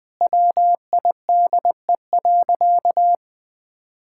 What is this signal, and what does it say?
Morse code, heard as a keyed tone, about 20 wpm, 700 Hz. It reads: WIDE.
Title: Sentences from Top 1000 Words - Review Part 6 - 20wpm